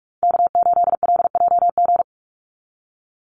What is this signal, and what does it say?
Morse code, heard as a keyed tone, about 46 wpm, 690 Hz. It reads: K8LJG
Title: NA SOTA Chaser Callsigns 45wpm